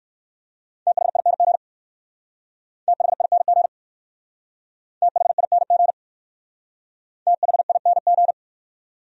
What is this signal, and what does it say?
Morse code, heard as a keyed tone, about 56 wpm, 700 Hz. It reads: THING THING THING THING
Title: Speed-Racing Top 200 Words 45wpm